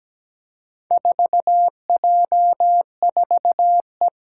Morse code, heard as a keyed tone, about 17 wpm, 700 Hz. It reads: 4J4E
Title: Four Challenging Letter-Number Combinations 17wpm